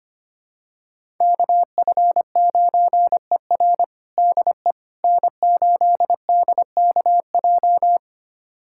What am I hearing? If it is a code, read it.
Morse code, heard as a keyed tone, about 25 wpm, 700 Hz. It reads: KF9ER DE N8DKJ
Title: USA Callsign DE Callsign 25wpm